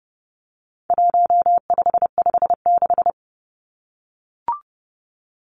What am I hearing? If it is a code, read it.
Morse code, heard as a keyed tone, about 30 wpm, 700 Hz. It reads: 1556
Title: Numbers - 4-Digit 30wpm